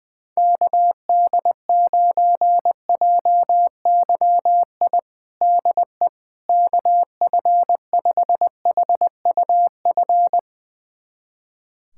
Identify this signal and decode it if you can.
Morse code, heard as a keyed tone, about 21 wpm, 700 Hz. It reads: KD9JYI DE KF5HUF
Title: USA Callsign DE Callsign 20wpm